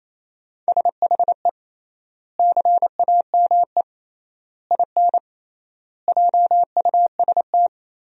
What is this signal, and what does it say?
Morse code, heard as a keyed tone, about 28 wpm, 700 Hz. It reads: SHE CAME IN JUST
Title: 3x Word-Spacing - Sentences from Top 300 Words - 28wpm